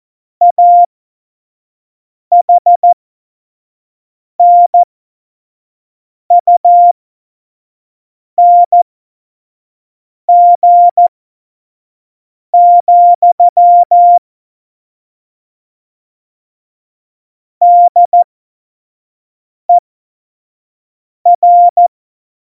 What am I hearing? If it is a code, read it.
Morse code, heard as a keyed tone, about 14 wpm, spaced as for 5 wpm, 700 Hz. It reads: AHNUNG, DER
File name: Morsekurs / Witze / Witz 19. Dezember 2017, 25 F70